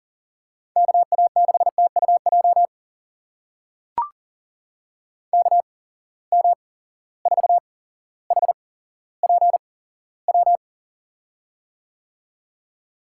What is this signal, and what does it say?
Morse code, heard as a keyed tone, about 37 wpm, spaced as for 17 wpm, 700 Hz. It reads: KA6TUJ KM4HPW
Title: USA Call Signs - 6 Characters - 8x Character Spacing - 40wpm